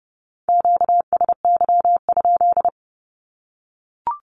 Morse code, heard as a keyed tone, about 30 wpm, 700 Hz. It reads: QSY?
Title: Q-Codes Repeated 3x 30wpm